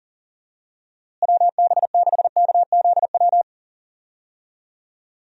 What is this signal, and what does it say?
Morse code, heard as a keyed tone, about 40 wpm, 700 Hz. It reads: WB6KZW
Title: USA Call Signs - 6 Characters - Repeated 3x - 40wpm